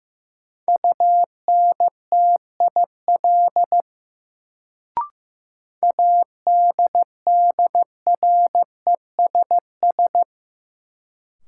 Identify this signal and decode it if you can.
Morse code, heard as a keyed tone, about 16 wpm, 700 Hz. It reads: UNTIL ADDRESS
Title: Sets of 5 Words Encore - New Word Drill - 15wpm